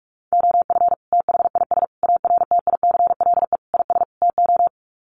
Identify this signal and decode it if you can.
Morse code, heard as a keyed tone, about 44 wpm, 690 Hz. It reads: OF THIS ARTICLE IS TO
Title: Sets of 5 Words Encore2 45wpm